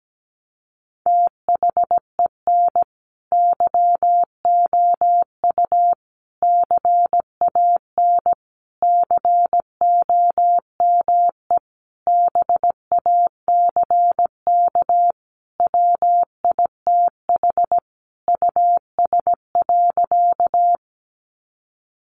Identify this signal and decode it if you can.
Morse code, heard as a keyed tone, about 17 wpm, 700 Hz. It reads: THEN YOU CAN COME BACK WITH US.
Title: A Sound Of Thunder 17wpm